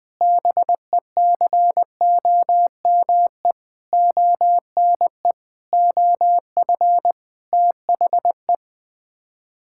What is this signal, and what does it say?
Morse code, heard as a keyed tone, about 20 wpm, 700 Hz. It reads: BECOME ONE OF THE
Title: Sets of 8 Words - 20wpm